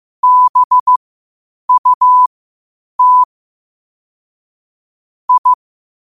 Morse code, heard as a keyed tone, about 15 wpm, spaced as for 8 wpm, 1000 Hz. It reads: BUT I
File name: qotdfiles / 8wpm